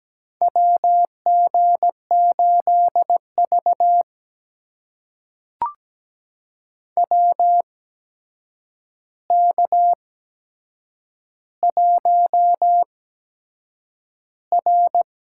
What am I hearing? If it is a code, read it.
Morse code, heard as a keyed tone, about 17 wpm, spaced as for 7 wpm, 700 Hz. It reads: WG8V WK1R
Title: USA Call Signs - 4 Characters - 8x Character Spacing - 17wpm